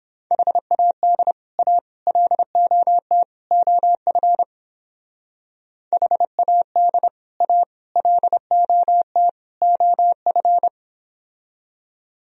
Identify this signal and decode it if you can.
Morse code, heard as a keyed tone, about 30 wpm, 700 Hz. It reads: HAD A LOT OF HAD A LOT OF